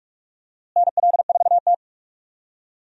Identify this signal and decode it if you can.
Morse code, heard as a keyed tone, about 46 wpm, 700 Hz. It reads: NL4T
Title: USA Call Signs - 4 Characters - Repeated 3x - 45wpm